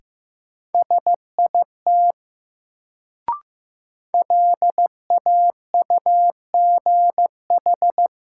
Morse code, heard as a keyed tone, about 15 wpm, 700 Hz. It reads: SIT LAUGH
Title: Sentences from Top 500 Words - New Word Drill - 15wpm